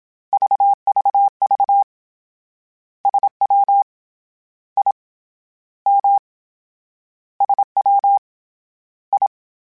Morse code, heard as a keyed tone, about 27 wpm, 800 Hz. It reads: VVV SW I M SW I